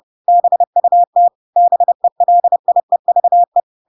Morse code, heard as a keyed tone, about 30 wpm, 700 Hz. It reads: BUT BELIEVE
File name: Pinocchio / Pinocchio ch7 30 30 ik1hge